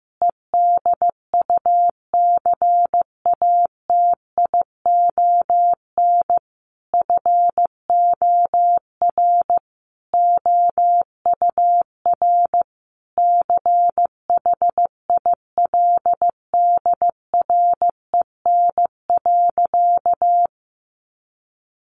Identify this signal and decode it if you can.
Morse code, heard as a keyed tone, about 15 wpm, 700 Hz. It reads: EDUCATION FOR OUR CHILDREN.
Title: Sentences from Top 1400 Words 15wpm